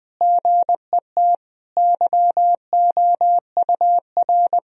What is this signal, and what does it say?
Morse code, heard as a keyed tone, about 20 wpm, 700 Hz. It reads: GET YOUR
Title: Sentences from Top 1000 Words - Review Part 4 - 20wpm